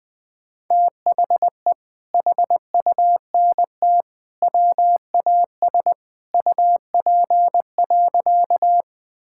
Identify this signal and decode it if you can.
Morse code, heard as a keyed tone, about 19 wpm, 700 Hz. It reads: THE HUNT WAS UP.
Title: Sentences from Top 1600 Words 20wpm